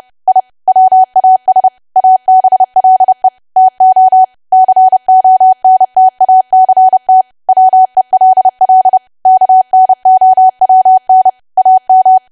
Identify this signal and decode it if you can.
Morse code, heard as a keyed tone, about 30 wpm, 750 Hz. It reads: I WAS ABLE TO CONTACT WELL KNOWN AM
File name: cwa / W1AW-Level3 / 30WPM